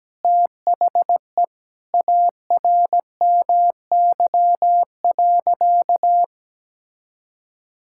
Morse code, heard as a keyed tone, about 17 wpm, 700 Hz. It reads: THE ARMY.
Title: Sentences from Top 1000 Words - Review Part 5 - 17wpm